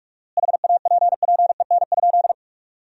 Morse code, heard as a keyed tone, about 41 wpm, 700 Hz. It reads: HAPPEN?
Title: 4x Word-Spacing - Sentences from Top 400 Words - 45wpm